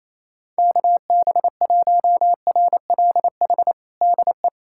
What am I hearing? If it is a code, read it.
Morse code, heard as a keyed tone, about 28 wpm, 700 Hz. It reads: KB1RLH DE
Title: USA Callsign DE Callsign 28wpm